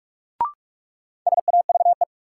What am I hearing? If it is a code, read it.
Morse code, heard as a keyed tone, about 46 wpm, 700 Hz. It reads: SAVE
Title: Sentences from Top 1100 Words - New Word Drill - 45wpm